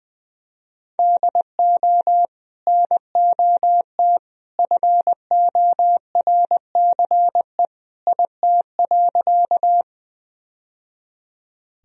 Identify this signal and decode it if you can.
Morse code, heard as a keyed tone, about 20 wpm, 700 Hz. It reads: DO NOT FORCE IT.
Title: Sentences from Top 1000 Words - Review Part 3 - 20wpm